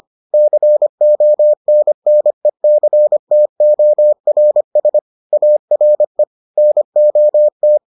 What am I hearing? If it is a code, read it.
Morse code, heard as a keyed tone, about 25 wpm, 600 Hz. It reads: CONNECTORS ARE NOT